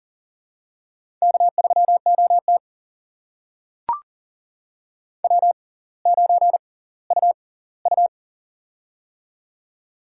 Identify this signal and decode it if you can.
Morse code, heard as a keyed tone, about 38 wpm, spaced as for 20 wpm, 700 Hz. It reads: K3OT W9UU
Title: USA Call Signs - 4 Characters - 6x Character Spacing - 40wpm